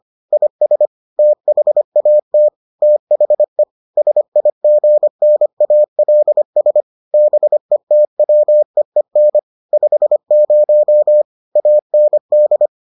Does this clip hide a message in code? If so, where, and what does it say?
Yes: Morse code from 0.3 to 12.7 s, heard as a keyed tone, about 25 wpm, 600 Hz. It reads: IS THAT THE SIGNALS BETWEEN 50 AND